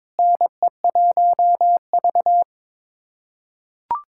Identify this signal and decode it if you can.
Morse code, heard as a keyed tone, about 21 wpm, 700 Hz. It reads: NE1V
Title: USA Callsign DE Callsign - 4 Characters - 22wpm